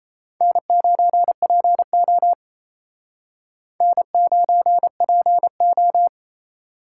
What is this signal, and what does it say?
Morse code, heard as a keyed tone, about 35 wpm, 700 Hz. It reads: N9PO N9PO
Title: Speed Racing - USA 1x2 Call Signs 28wpm